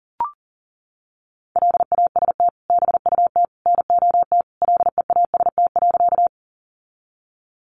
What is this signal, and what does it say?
Morse code, heard as a keyed tone, about 38 wpm, 700 Hz. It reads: LAST BUT NOT LEAST.
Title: Sentences from Top 700 Words 40wpm